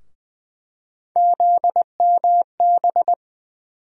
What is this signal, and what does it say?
Morse code, heard as a keyed tone, about 20 wpm, 700 Hz. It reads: ZMB
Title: USA Call Sign Postfixes - 3 Letters 20wpm